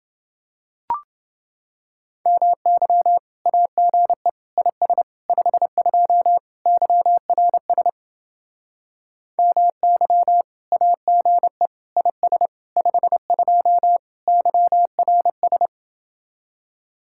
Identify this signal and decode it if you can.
Morse code, heard as a keyed tone, about 28 wpm, 700 Hz. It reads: MY AGE IS 52 YRS MY AGE IS 52 YRS